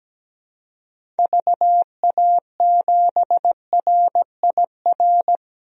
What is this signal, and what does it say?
Morse code, heard as a keyed tone, about 17 wpm, 700 Hz. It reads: VA7RIR
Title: DXCC-100 Call Signs - 6 Characters - 17wpm